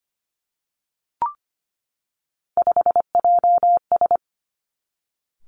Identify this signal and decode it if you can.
Morse code, heard as a keyed tone, about 25 wpm, 700 Hz. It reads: 5JS
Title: Three Challenging Letter-Number Combinations 25wpm